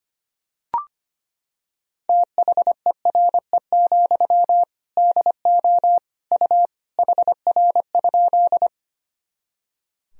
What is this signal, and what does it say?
Morse code, heard as a keyed tone, about 25 wpm, 700 Hz. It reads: THERE, DO U HR?
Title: Sentences from Top 300 Words and Abbreviations 25wpm